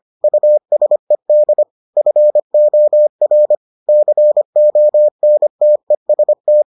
Morse code, heard as a keyed tone, about 25 wpm, 600 Hz. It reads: USED FOR CONTEST